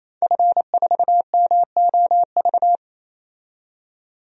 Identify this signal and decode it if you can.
Morse code, heard as a keyed tone, about 28 wpm, 700 Hz. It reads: F4MOV
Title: DXCC-100 Call Signs - 5 Characters - 28wpm